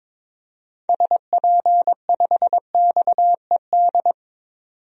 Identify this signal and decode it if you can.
Morse code, heard as a keyed tone, about 22 wpm, 700 Hz. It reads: SP5XED
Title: DXCC-100 Call Signs - 6 Characters - 22wpm